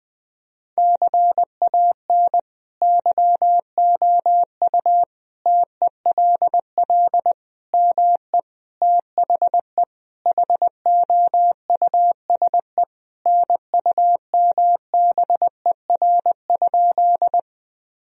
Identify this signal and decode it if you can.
Morse code, heard as a keyed tone, about 20 wpm, 700 Hz. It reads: CAN YOU TELL ME THE HOUSE NUMBER?
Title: Sentences from Top 1000 Words - Review Part 1 - 20wpm